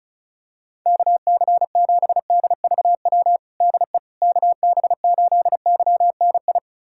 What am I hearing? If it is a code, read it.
Morse code, heard as a keyed tone, about 34 wpm, 700 Hz. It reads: KC7DVW DE KB8YNI
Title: USA Callsign DE Callsign 35wpm